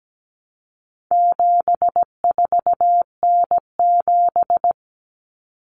Morse code, heard as a keyed tone, about 17 wpm, 700 Hz. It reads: 74N7
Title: Common Cut Numbers - 4-Digit 17wpm